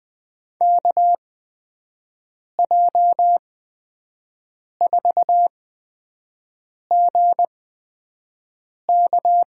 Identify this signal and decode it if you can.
Morse code, heard as a keyed tone, about 20 wpm, spaced as for 5 wpm, 700 Hz. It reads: KJ4GK
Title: USA Call Signs - 8x Character Spacing - 20wpm